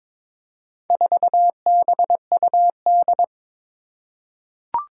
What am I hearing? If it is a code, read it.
Morse code, heard as a keyed tone, about 22 wpm, 700 Hz. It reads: 4BUD
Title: Four Challenging Letter-Number Combinations 22wpm